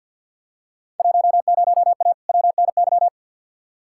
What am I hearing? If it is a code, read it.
Morse code, heard as a keyed tone, about 48 wpm, 700 Hz. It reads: 10A WNY
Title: Field Day Exchanges - 50wpm